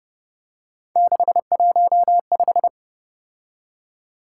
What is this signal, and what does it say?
Morse code, heard as a keyed tone, about 29 wpm, 700 Hz. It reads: 615